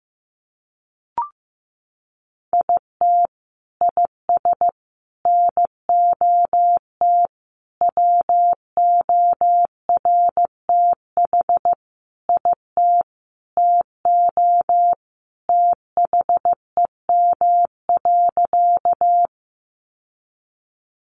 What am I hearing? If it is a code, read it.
Morse code, heard as a keyed tone, about 15 wpm, 700 Hz. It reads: IT IS NOT WORTH IT TO THEM.
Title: Sentences from Top 800 Words 15wpm